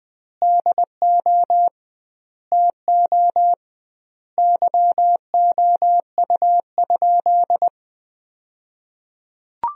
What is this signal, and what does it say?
Morse code, heard as a keyed tone, about 21 wpm, 700 Hz. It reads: DO TO YOU?
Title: 2x Word-Spacing - Sentences from Top 100 Words - 20wpm